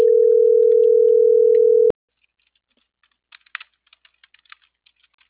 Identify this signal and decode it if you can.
Ringback tone (North America), 440 + 480 Hz.